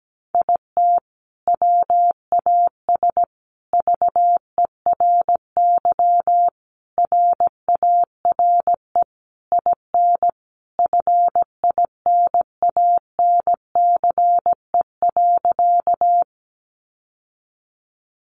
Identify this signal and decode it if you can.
Morse code, heard as a keyed tone, about 17 wpm, 700 Hz. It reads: IT WAS VERY RARE IN FINANCE.
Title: Sentences from Top 1500 Words 17wpm